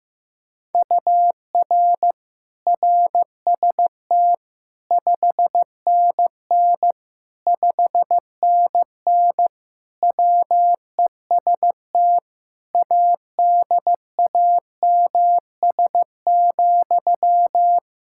Morse code, heard as a keyed tone, about 15 wpm, 700 Hz. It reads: UR RST 5NN 5NN WEST ADAMS,
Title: K1USN SST QSO Practice 15wpm